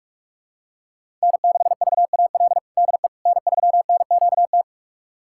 Morse code, heard as a keyed tone, about 47 wpm, 700 Hz. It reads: N6VAL DE N3NQT
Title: USA Callsign DE Callsign - 5 Characters - 45wpm